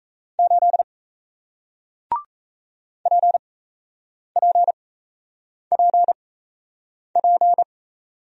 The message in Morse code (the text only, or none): 8 PPPP